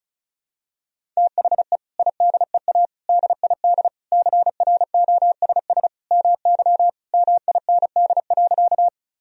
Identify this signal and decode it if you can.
Morse code, heard as a keyed tone, about 36 wpm, 700 Hz. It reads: THE IDEA DID CROSS MY MIND.
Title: Sentences from Top 1000 Words - Review Part 1 - 35wpm